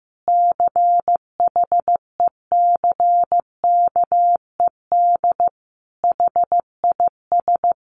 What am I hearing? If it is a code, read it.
Morse code, heard as a keyed tone, about 15 wpm, 700 Hz. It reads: CHECKED HIS
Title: A Sound Of Thunder 15wpm